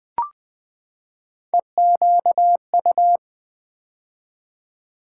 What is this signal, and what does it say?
Morse code, heard as a keyed tone, about 20 wpm, 700 Hz. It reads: EQU
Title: Build A Word - 3 Letters At A Time - 8-Letter Words - 20wpm